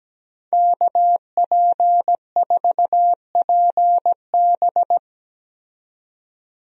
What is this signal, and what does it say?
Morse code, heard as a keyed tone, about 17 wpm, 700 Hz. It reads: KP4PB
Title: DXCC-100 Call Signs - 5 Characters - 17wpm